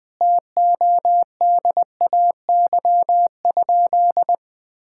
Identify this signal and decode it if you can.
Morse code, heard as a keyed tone, about 20 wpm, 700 Hz. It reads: TODAY?
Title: Sentences with 4-Word Phrases - 20wpm